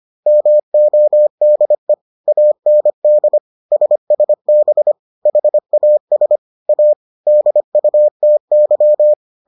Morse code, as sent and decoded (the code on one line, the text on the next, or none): -- --- -.. / .- -. -.. / ... ... -... / .... .- ... / .- / -.. ..- - -.--
MODE AND SSB HAS A DUTY